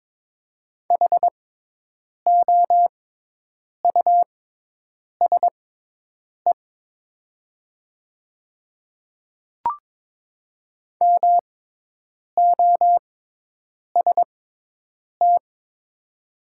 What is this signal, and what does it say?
Morse code, heard as a keyed tone, about 22 wpm, spaced as for 8 wpm, 700 Hz. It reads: HOUSE MOST
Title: 6x Character Spacing - Top 200 Words - 22wpm